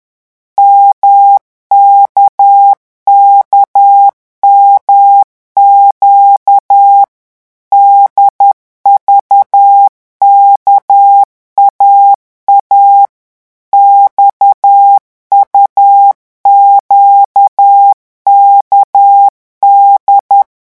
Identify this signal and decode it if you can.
Morse code, heard as a keyed tone, about 10 wpm, 800 Hz. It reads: MKKMQ DVKAA XUQKD
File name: morse mp3 / revision3-vitesse 10 mots